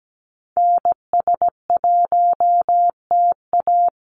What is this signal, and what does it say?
Morse code, heard as a keyed tone, about 17 wpm, 700 Hz. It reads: NS1TA
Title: NA SOTA Activator Callsigns 17wpm